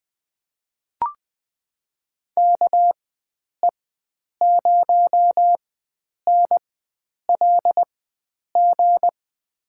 Morse code, heard as a keyed tone, about 20 wpm, spaced as for 9 wpm, 700 Hz. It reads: KE0NLG
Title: USA Call Signs - 4x Character Spacing - 20wpm